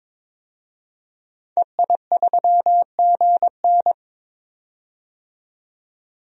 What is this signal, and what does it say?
Morse code, heard as a keyed tone, about 22 wpm, 700 Hz. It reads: EI3GN
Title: DXCC-100 Call Signs - 5 Characters - 22wpm